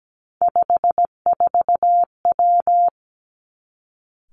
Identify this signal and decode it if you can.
Morse code, heard as a keyed tone, about 17 wpm, 700 Hz. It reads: 54W
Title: Three Challenging Letter-Number Combinations 17wpm